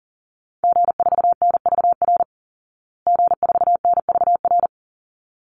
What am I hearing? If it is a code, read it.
Morse code, heard as a keyed tone, about 40 wpm, 700 Hz. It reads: G4NVR G4NVR
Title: Call Signs Repeated 3x - 40wpm